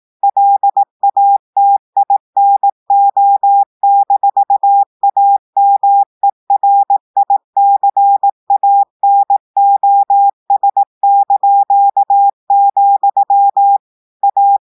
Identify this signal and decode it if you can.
Morse code, heard as a keyed tone, about 18 wpm, 800 Hz. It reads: LATINO-AMERICANOS), A
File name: audios / portugais / portugais texte en clair 1 vitesse 18 mots minutes